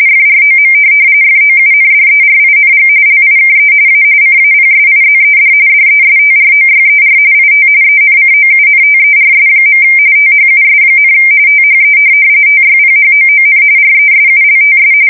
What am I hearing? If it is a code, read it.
RTTY, about 45.45 baud. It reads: FROM FOUNDERING LIKE A CAPSIZED DESTROYER IT ENCOUNTERS. THUS, THE TYPHOON RESULTS
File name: fsk170 / USS-New-Jersey